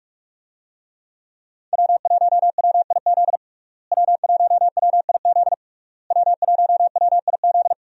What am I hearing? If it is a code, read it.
Morse code, heard as a keyed tone, about 46 wpm, 700 Hz. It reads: W1WIZ W1WIZ W1WIZ
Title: USA Call Signs - 5 Characters - Repeated 3x - 45wpm